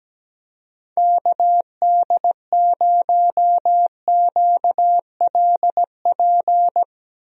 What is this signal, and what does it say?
Morse code, heard as a keyed tone, about 17 wpm, 700 Hz. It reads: KD0QLP
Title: USA Call Signs - 6 Characters - Repeated 3x - 17wpm